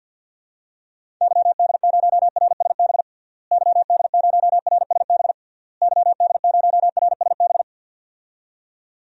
Morse code, heard as a keyed tone, about 50 wpm, 700 Hz. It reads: YD0RSB YD0RSB YD0RSB
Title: DXCC-100 Call Signs - 6 Characters - Repeated 3x - 50wpm